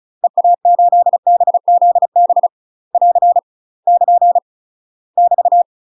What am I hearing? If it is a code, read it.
Morse code, heard as a keyed tone, about 36 wpm, 700 Hz. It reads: EA8BZB <AR> <KN> <BT>
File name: UK MASTER QSOs TRAINING / 35WPM / UK MASTER QSOs 0020